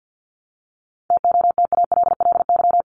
Morse code, heard as a keyed tone, about 48 wpm, 700 Hz. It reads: TOTALLY
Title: LY Suffix - Top 100 Words with Base Words 50wpm